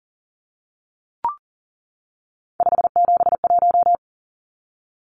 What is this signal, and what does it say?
Morse code, heard as a keyed tone, about 40 wpm, 700 Hz. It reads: 571
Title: Numbers - 3-Digit 40wpm